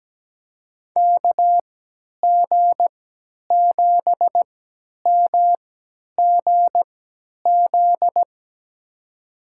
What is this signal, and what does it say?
Morse code, heard as a keyed tone, about 17 wpm, spaced as for 10 wpm, 700 Hz. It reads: KG7MGZ
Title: USA Call Signs - 6 Characters - 3x Character Spacing - 17wpm